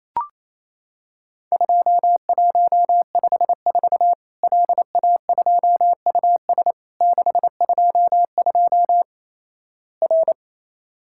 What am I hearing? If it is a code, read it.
Morse code, heard as a keyed tone, about 28 wpm, 700 Hz. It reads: 2154 LA2US 622 R
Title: Worked All Europe DX Contest - QTC Exchange - 28wpm